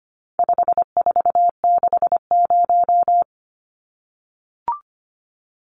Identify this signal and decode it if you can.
Morse code, heard as a keyed tone, about 25 wpm, 700 Hz. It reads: 5460
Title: Numbers - 4-Digit 25wpm